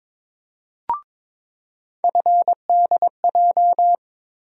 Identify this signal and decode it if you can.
Morse code, heard as a keyed tone, about 22 wpm, 700 Hz. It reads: FDJ
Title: USA Call Sign Postfixes - 3 Letters 22wpm